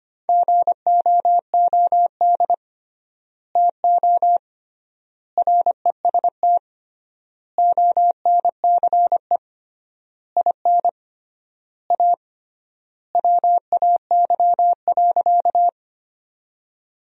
Morse code, heard as a keyed tone, about 25 wpm, 700 Hz. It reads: GOOD TO REST ONCE IN A WAY.
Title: 3x Word-Spacing - Sentences from Top 500 Words - 25wpm